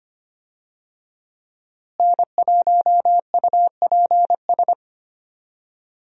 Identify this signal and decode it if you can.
Morse code, heard as a keyed tone, about 25 wpm, 700 Hz. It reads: N1UPS
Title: USA Call Signs - 5 Characters - Repeated 3x - 25wpm